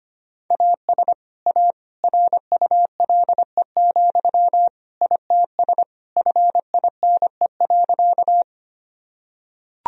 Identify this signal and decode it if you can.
Morse code, heard as a keyed tone, about 25 wpm, 700 Hz. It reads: AS A RULE, ITS FINE.
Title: Sentences from Top 1000 Words - Review Part 2 - 25wpm